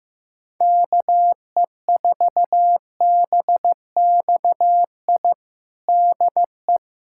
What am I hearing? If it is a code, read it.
Morse code, heard as a keyed tone, about 15 wpm, 700 Hz. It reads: KE4BXI DE
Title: USA Callsign DE Callsign 15wpm